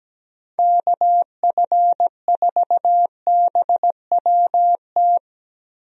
Morse code, heard as a keyed tone, about 17 wpm, 700 Hz. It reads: KF4BWT
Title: USA Call Signs - 6 Characters - Repeated 3x - 17wpm